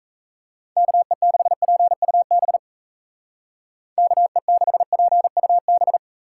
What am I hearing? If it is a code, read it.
Morse code, heard as a keyed tone, about 34 wpm, 700 Hz. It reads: KE6PUB KE6PUB
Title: Speed Racing - USA Call Signs 28wpm